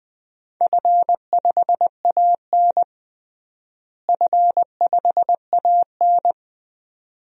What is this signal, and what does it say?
Morse code, heard as a keyed tone, about 20 wpm, 700 Hz. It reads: F5AN F5AN
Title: DXCC-100 Call Signs - 4 Characters - Repeated 2x - 20wpm